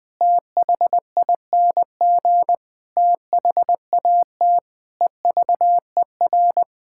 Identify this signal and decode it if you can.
Morse code, heard as a keyed tone, about 20 wpm, 700 Hz. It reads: THING THAT EVER
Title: Sets of 8 Words - 20wpm